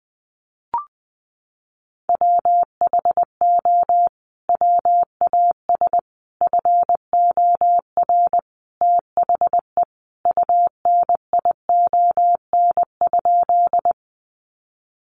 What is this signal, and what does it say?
Morse code, heard as a keyed tone, about 20 wpm, 700 Hz. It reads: WHO WAS FOR THE UNION?
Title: Sentences from Top 1400 Words 20wpm